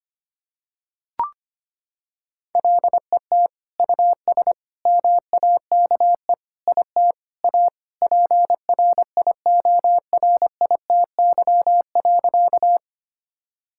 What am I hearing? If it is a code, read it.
Morse code, heard as a keyed tone, about 25 wpm, 700 Hz. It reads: LET US MAKE IT A PRIORITY.
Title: Sentences from Top 1600 Words 25wpm